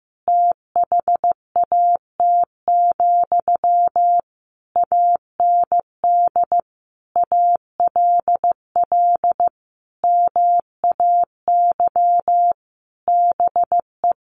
Morse code, heard as a keyed tone, about 15 wpm, 700 Hz. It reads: THAT, AND ALL MAY BE
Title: Sentences from Top 1200 Words 15wpm